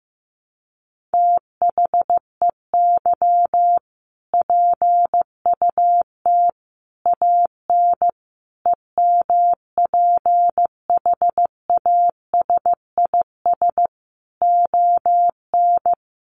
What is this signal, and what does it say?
Morse code, heard as a keyed tone, about 15 wpm, 700 Hz. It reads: THEY PUT AN EMPHASIS ON
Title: Sentences from Top 1400 Words 15wpm